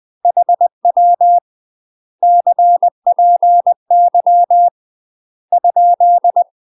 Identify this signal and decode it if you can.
Morse code, heard as a keyed tone, about 20 wpm, 700 Hz. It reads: HW CPY ?
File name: UK MASTER QSOs TRAINING / 20WPM / UK MASTER QSOs 0036